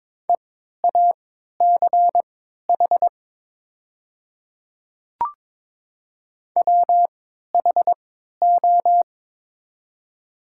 Morse code, heard as a keyed tone, about 22 wpm, spaced as for 12 wpm, 700 Hz. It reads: EACH WHO
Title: 3x Character Spacing - Top 100 Words - 22wpm